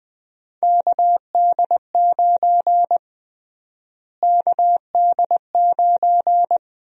Morse code, heard as a keyed tone, about 21 wpm, 700 Hz. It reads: KD9 KD9
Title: USA Call Sign Prefixes - Repeated 2x - 20wpm